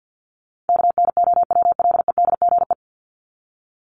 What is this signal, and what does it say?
Morse code, heard as a keyed tone, about 50 wpm, 700 Hz. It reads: KNOWLEDGE
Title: Top 501-750 Words 50wpm